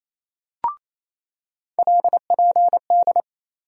Morse code, heard as a keyed tone, about 28 wpm, 700 Hz. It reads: LPD
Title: USA Call Sign Postfixes - 3 Letters 28wpm